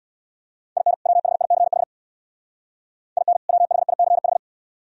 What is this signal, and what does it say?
Morse code, heard as a keyed tone, about 62 wpm, 700 Hz. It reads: IN WHICH IN WHICH
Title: Speed-Racing - Sets of 2 Words 50wpm